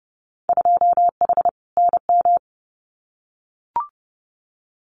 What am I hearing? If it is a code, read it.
Morse code, heard as a keyed tone, about 30 wpm, 700 Hz. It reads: 2H NM
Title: Winter Field Day Exchanges Encore 30wpm